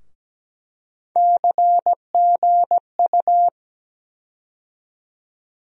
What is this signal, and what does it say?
Morse code, heard as a keyed tone, about 17 wpm, 700 Hz. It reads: CGU
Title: USA Call Sign Postfixes - 3 Letters 17wpm